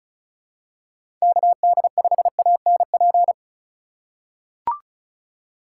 Morse code, heard as a keyed tone, about 36 wpm, 700 Hz. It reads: KD5ANP
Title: USA Call Signs - 3x Character Spacing - 35wpm